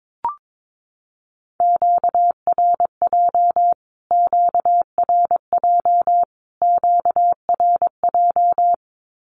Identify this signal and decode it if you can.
Morse code, heard as a keyed tone, about 22 wpm, 700 Hz. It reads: QRJ QRJ QRJ
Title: Q-Codes Repeated 3x 22wpm